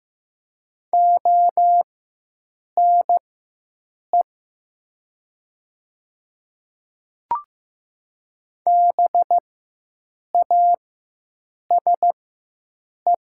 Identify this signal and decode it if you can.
Morse code, heard as a keyed tone, about 15 wpm, spaced as for 7 wpm, 700 Hz. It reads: ONE BASE
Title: 4x Character Spacing - Top 500 Words - 15wpm